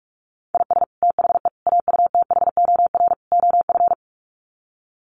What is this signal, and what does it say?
Morse code, heard as a keyed tone, about 44 wpm, 700 Hz. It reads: IS THE AUTHOR OF
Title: Sets of 4 Words 45wpm